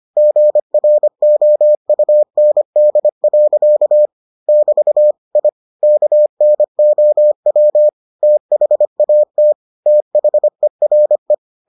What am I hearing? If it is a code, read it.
Morse code, heard as a keyed tone, about 25 wpm, 600 Hz. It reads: GROUND. <BT> I KNOW THAT THERE